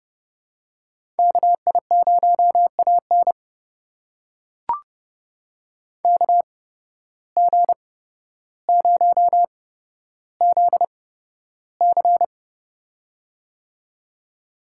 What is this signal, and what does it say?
Morse code, heard as a keyed tone, about 31 wpm, spaced as for 13 wpm, 700 Hz. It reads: KI0AN KG0ZC
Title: USA Call Signs - 5 Characters - 8x Character Spacing - 30wpm